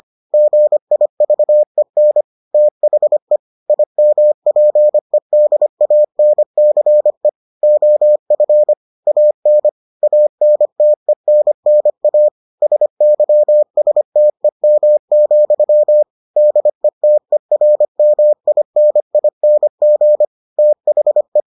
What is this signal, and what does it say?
Morse code, heard as a keyed tone, about 25 wpm, 600 Hz. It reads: GIVEN THE IMPEDANCE OF AN ANTENNA SYSTEM, DETERMINING THE